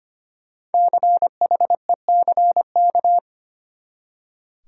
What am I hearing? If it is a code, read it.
Morse code, heard as a keyed tone, about 31 wpm, 700 Hz. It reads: CHECK
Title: Speed-Racing Top 500 Words 25wpm